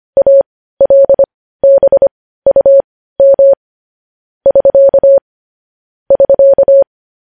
Morse code, heard as a keyed tone, about 25 wpm, spaced as for 15 wpm, 550 Hz. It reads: ALBUM <SK> <SK>
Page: news-headlines-in-morse-code-at-15-wpm